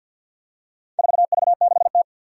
Morse code, heard as a keyed tone, about 53 wpm, 700 Hz. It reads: 4V6T
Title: Four Challenging Letter-Number Combinations 50wpm